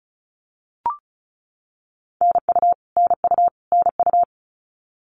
Morse code, heard as a keyed tone, about 34 wpm, 700 Hz. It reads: NU NU NU
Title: Canadian Provinces and Territories - Repeated 3x - 35wpm